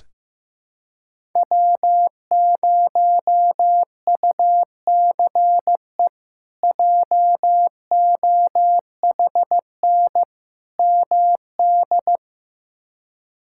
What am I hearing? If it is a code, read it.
Morse code, heard as a keyed tone, about 15 wpm, 700 Hz. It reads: W0UCE JOHN MD
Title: K1USN SST QSO Practice - 15wpm